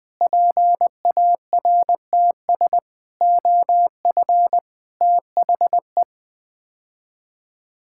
Morse code, heard as a keyed tone, about 20 wpm, 700 Hz. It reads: PARTS OF THE
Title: Build A Phrase - 5-Word Phrases - 20wpm